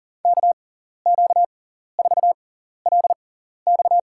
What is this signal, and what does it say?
Morse code, heard as a keyed tone, about 42 wpm, spaced as for 14 wpm, 700 Hz. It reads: KQ4LX
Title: USA Call Signs - 5 Characters - 6x Character Spacing - 40wpm